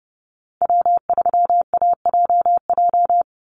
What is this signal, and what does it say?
Morse code, heard as a keyed tone, about 30 wpm, 700 Hz. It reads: W3AJJ
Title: USA Call Signs 30wpm